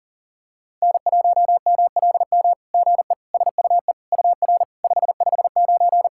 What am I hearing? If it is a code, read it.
Morse code, heard as a keyed tone, about 38 wpm, 690 Hz. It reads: N1MLM GE SUE UR 559
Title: SOTA QSO Practice 40wpm